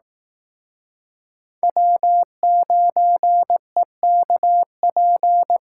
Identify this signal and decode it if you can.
Morse code, heard as a keyed tone, about 18 wpm, 700 Hz. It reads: W9EKP
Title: Speed Racing - USA 1x3 Call Signs 15wpm